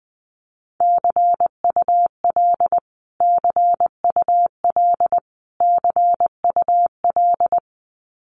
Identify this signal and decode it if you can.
Morse code, heard as a keyed tone, about 20 wpm, 700 Hz. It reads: CUL CUL CUL
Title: Abbreviations Repeated 3x - 20wpm